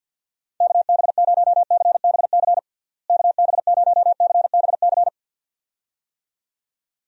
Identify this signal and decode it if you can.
Morse code, heard as a keyed tone, about 51 wpm, 700 Hz. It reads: KB0CBC KB0CBC
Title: USA Call Signs - 6 Characters - Repeated 2x - 50wpm